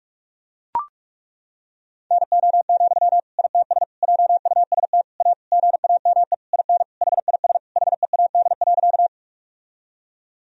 Morse code, heard as a keyed tone, about 41 wpm, 700 Hz. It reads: NO, ITS JUST A GAME IN HIS HEAD.
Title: Sentences from Top 1000 Words - Review Part 3 - 45wpm